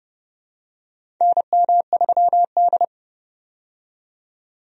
Morse code, heard as a keyed tone, about 30 wpm, 700 Hz. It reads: NM3D
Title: USA Call Signs - 4 Characters - 30wpm